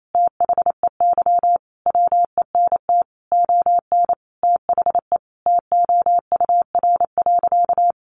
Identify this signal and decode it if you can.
Morse code, heard as a keyed tone, about 28 wpm, 700 Hz. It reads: THEY WENT ON THE TOUR.
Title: Sentences from Top 1400 Words 28wpm